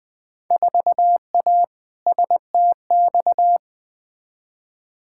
Morse code, heard as a keyed tone, about 20 wpm, 700 Hz. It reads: 4A STX
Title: Field Day Exchanges - 20wpm